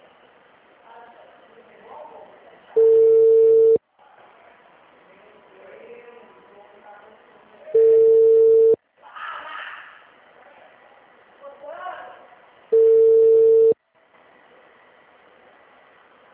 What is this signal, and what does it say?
Ringback tone (China, 450 Hz), with speech.
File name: out- -2001- - - .154